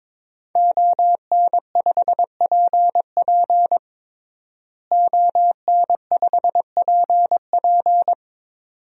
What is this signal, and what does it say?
Morse code, heard as a keyed tone, about 22 wpm, 700 Hz. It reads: ON5PP ON5PP
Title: DXCC-100 Call Signs - 5 Characters - Repeated 2x - 22wpm